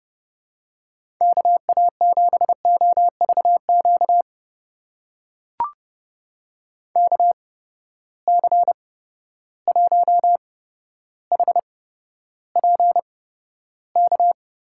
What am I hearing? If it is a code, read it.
Morse code, heard as a keyed tone, about 30 wpm, spaced as for 13 wpm, 700 Hz. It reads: KA7OVQ KC1HPK
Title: USA Call Signs - 6 Characters - 8x Character Spacing - 30wpm